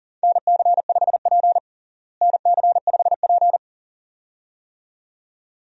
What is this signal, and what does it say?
Morse code, heard as a keyed tone, about 40 wpm, 700 Hz. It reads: NC5P NC5P
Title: USA Call Signs - 4 Characters - Repeated 2x - 40wpm